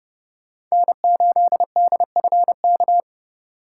Morse code, heard as a keyed tone, about 38 wpm, 700 Hz. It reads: N8DFK